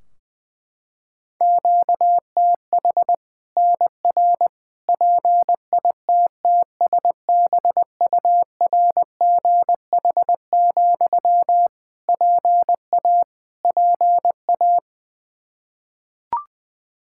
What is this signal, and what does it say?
Morse code, heard as a keyed tone, about 21 wpm, 700 Hz. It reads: QTH NR PITTSBURGH, PA PA
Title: Practice QSO - Exchange 1 - Beginning - 20wpm